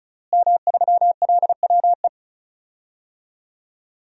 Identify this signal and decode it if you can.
Morse code, heard as a keyed tone, about 36 wpm, 700 Hz. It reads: M3LWE
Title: DXCC-100 Call Signs - 5 Characters - Repeated 2x - 35wpm